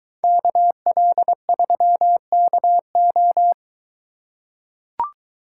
Morse code, heard as a keyed tone, about 22 wpm, 700 Hz. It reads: KL3KO